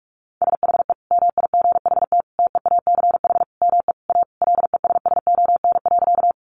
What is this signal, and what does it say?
Morse code, heard as a keyed tone, about 44 wpm, 700 Hz. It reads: SHE MIGHT TEACH ME A LESSON.
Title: Sentences from Top 1200 Words 45wpm